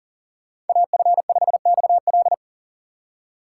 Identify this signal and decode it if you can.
Morse code, heard as a keyed tone, about 49 wpm, 700 Hz. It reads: AF5XL
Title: Speed Racing - USA 2x2 Call Signs 40wpm